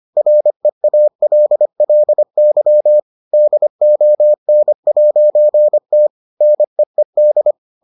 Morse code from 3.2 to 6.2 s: DON'T

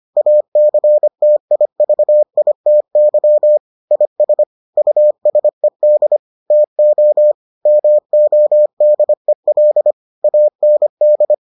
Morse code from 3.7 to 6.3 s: IS USED

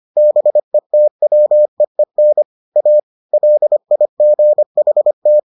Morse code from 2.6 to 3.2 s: A

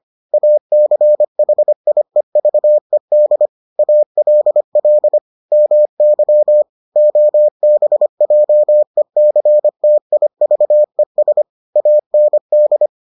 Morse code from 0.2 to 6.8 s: ACHIEVED ALL MY

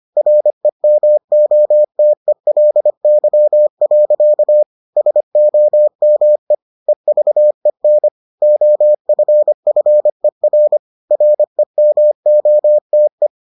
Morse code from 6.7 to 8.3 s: EVEN

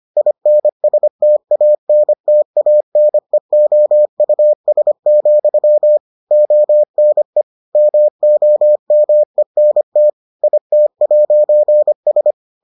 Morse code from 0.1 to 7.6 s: INSTANTANEOUS, ONE

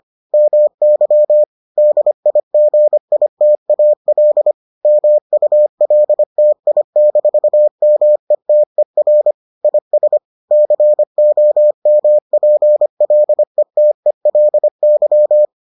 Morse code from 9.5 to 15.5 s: IS COMPLETELY

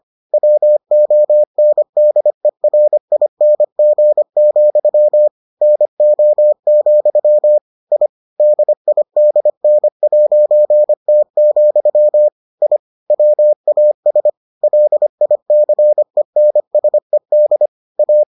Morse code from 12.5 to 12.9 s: I

